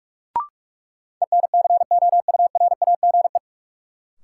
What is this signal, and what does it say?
Morse code, heard as a keyed tone, about 42 wpm, 700 Hz. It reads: ENCOURAGE
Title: Sentences from Top 1400 Words - New Word Drill Alt - 45wpm